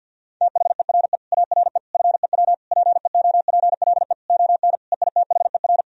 Morse code, heard as a keyed tone, about 50 wpm, 700 Hz. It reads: THERE ARE FEW PEOPLE ON EITHER
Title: Sentences from Top 1000 Words - Review Part 4 - 50wpm